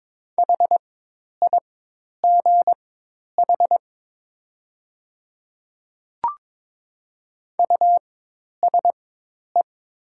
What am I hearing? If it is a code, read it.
Morse code, heard as a keyed tone, about 22 wpm, spaced as for 10 wpm, 700 Hz. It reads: HIGH USE
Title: 4x Character Spacing - Top 300 Words - 22wpm